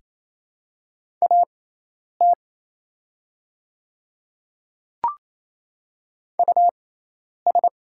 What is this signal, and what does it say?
Morse code, heard as a keyed tone, about 28 wpm, spaced as for 10 wpm, 700 Hz. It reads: AT US
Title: 6x Character Spacing - Top 300 Words - 28wpm